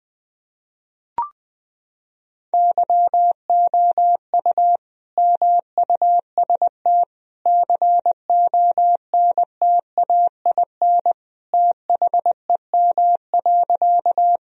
Morse code, heard as a keyed tone, about 20 wpm, 700 Hz. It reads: YOU MUST CONTAIN THEM.
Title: Sentences from Top 1000 Words - Review Part 2 - 20wpm